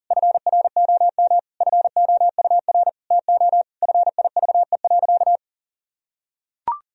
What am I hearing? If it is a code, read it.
Morse code, heard as a keyed tone, about 40 wpm, 700 Hz. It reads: FROM FOUR TO FIVE.
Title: Sentences from Top 1000 Words - Review Part 2 - 40wpm